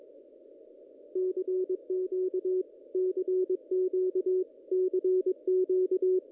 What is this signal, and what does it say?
Morse code, heard as a keyed tone, about 22 wpm, 370 Hz. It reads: CQ CQ CQ
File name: contests / ss-cw-k5tr-op-n5rz -1800